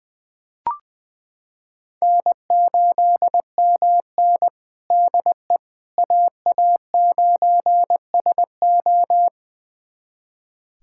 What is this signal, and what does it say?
Morse code, heard as a keyed tone, about 20 wpm, 700 Hz. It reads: N8MN DE AA9SO
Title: USA Callsign DE Callsign 20wpm